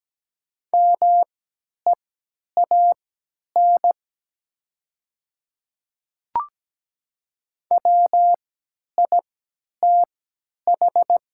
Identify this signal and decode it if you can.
Morse code, heard as a keyed tone, about 17 wpm, spaced as for 10 wpm, 690 Hz. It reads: MEAN WITH
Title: 3x Character Spacing - Top 300 Words - 17wpm